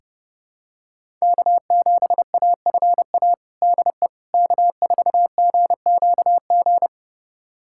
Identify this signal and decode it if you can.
Morse code, heard as a keyed tone, about 30 wpm, 700 Hz. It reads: K7AFA DE K4GQG
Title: USA Callsign DE Callsign - 5 Characters - 30wpm